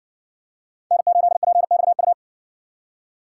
Morse code, heard as a keyed tone, about 60 wpm, 700 Hz. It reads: N8PXU